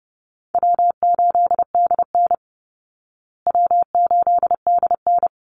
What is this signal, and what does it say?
Morse code, heard as a keyed tone, about 30 wpm, 700 Hz. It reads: W8DN W8DN
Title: Easy Call Signs Repeated 3x - 30wpm